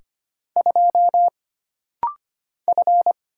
Morse code, heard as a keyed tone, about 25 wpm, 700 Hz. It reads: 2 F
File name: Single Character (Letter:Number:Punctuation:Prosign) - Rapid-Fire - 25wpm